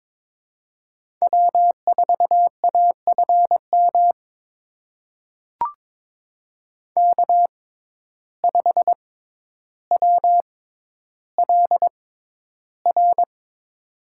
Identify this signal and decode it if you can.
Morse code, heard as a keyed tone, about 22 wpm, spaced as for 11 wpm, 700 Hz. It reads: W4AFM K5WLR
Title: USA Call Signs - 5 Characters - 6x Character Spacing - 22wpm